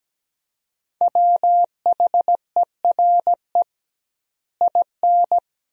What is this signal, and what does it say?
Morse code, heard as a keyed tone, about 17 wpm, 700 Hz. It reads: WHERE IN
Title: 2x Word-Spacing - Sentences from Top 200 Words - 17wpm